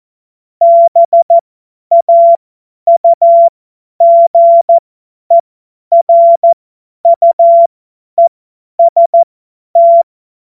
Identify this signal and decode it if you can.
Morse code, heard as a keyed tone, about 14 wpm, spaced as for 10 wpm, 680 Hz. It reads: BAUGERUEST
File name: Morsekurs / Witze / Witz 16. Dezember 2025 50 F70